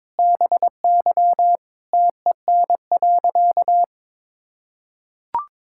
Morse code, heard as a keyed tone, about 22 wpm, 700 Hz. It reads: BY TEN.
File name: practice / 22wpm / Sentences with 4-Word Phrases - 22wpm